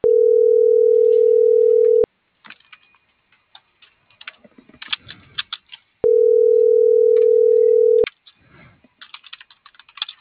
Ringback tone (North America), 440 + 480 Hz.